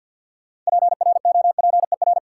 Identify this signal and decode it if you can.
Morse code, heard as a keyed tone, about 46 wpm, 700 Hz. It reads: PROPER
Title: Sentences from Top 800 Words - New Word Drill - 50wpm